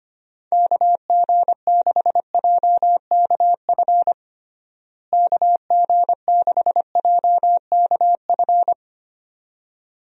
Morse code, heard as a keyed tone, about 25 wpm, 700 Hz. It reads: KG6JKF KG6JKF
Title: USA Call Signs - 6 Characters - Repeated 2x - 25wpm